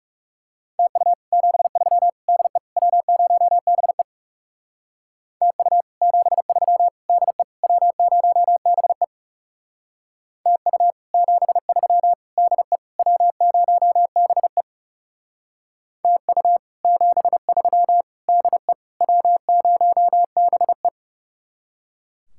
Morse code, heard as a keyed tone, about 38 wpm, 700 Hz. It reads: TU 73 DE W0BE TU 73 DE W0BE TU 73 DE W0BE TU 73 DE W0BE